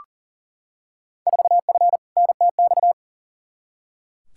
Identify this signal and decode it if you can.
Morse code, heard as a keyed tone, about 40 wpm, 700 Hz. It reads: 4F NTX
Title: Field Day Exchanges Encore - 40wpm